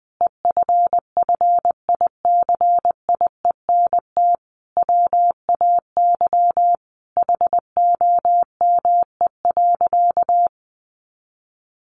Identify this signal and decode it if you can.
Morse code, heard as a keyed tone, about 20 wpm, 700 Hz. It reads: EFFICIENT WAY HOME.
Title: Sentences from Top 1300 Words 20wpm